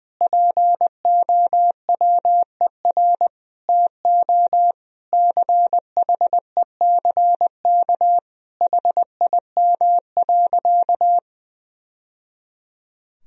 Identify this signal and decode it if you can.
Morse code, heard as a keyed tone, about 20 wpm, 700 Hz. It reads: POWER TO CHECK HIM.
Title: Sentences from Top 1000 Words - Review Part 3 - 20wpm